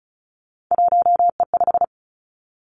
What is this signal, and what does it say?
Morse code, heard as a keyed tone, about 36 wpm, 700 Hz. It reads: 1E5
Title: Three Challenging Letter-Number Combinations 35wpm